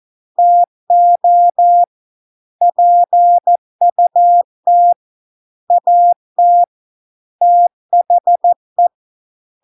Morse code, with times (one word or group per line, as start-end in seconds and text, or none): 0.4-1.8 TO
2.6-4.9 PUT
5.7-6.6 AT
7.4-8.9 THE